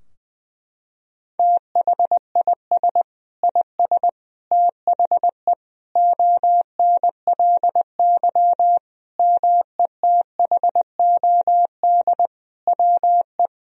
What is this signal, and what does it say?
Morse code, heard as a keyed tone, about 20 wpm, 700 Hz. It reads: THIS IS THE ONLY METHOD WE
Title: Sentences from Top 1000 Words - Review Part 3 - 20wpm